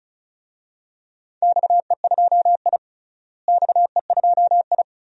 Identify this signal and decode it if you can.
Morse code, heard as a keyed tone, about 36 wpm, 700 Hz. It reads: XE2I XE2I
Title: DXCC-100 Call Signs - 4 Characters - Repeated 2x - 35wpm